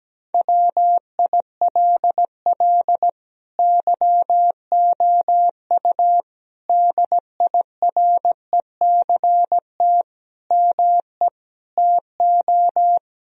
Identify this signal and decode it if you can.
Morse code, heard as a keyed tone, about 17 wpm, 700 Hz. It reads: WILL YOU DIRECT ME TO
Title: Sentences from Top 1000 Words - Review Part 2 - 17wpm